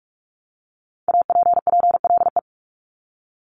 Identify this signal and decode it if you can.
Morse code, heard as a keyed tone, about 46 wpm, 700 Hz. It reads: APPLE
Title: Multi-letter Phonemes Repeated 3x 45wpm